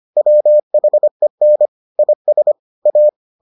Morse code, heard as a keyed tone, about 25 wpm, 600 Hz. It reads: WHEN IS A